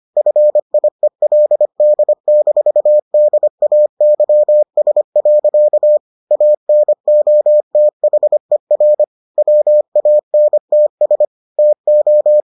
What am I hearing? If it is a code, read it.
Morse code, heard as a keyed tone, about 25 wpm, 600 Hz. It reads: FIELD-DAYS. ANOTHER WANTS TO